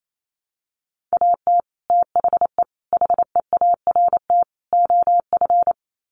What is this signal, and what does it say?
Morse code, heard as a keyed tone, about 28 wpm, 700 Hz. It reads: AT THE HEART OF
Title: Sets of 4 Words 28wpm